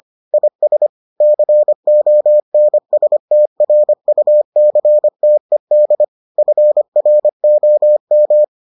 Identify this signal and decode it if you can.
Morse code, heard as a keyed tone, about 25 wpm, 600 Hz. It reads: IS CONSTRUCTED FROM